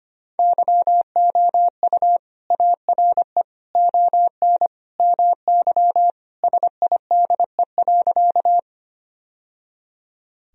Morse code, heard as a keyed tone, about 25 wpm, 700 Hz. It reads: YOU ARE ON MY SIDE.
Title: Sentences from Top 1000 Words - Review Part 1 - 25wpm